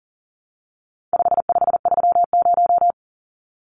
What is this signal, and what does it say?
Morse code, heard as a keyed tone, about 40 wpm, 700 Hz. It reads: 5530
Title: Numbers - 4-Digit 40wpm